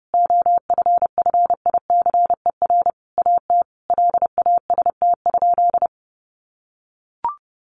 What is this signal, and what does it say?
Morse code, heard as a keyed tone, about 30 wpm, 700 Hz. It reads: OFFICER AT LAST?
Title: Sentences from Top 1300 Words 30wpm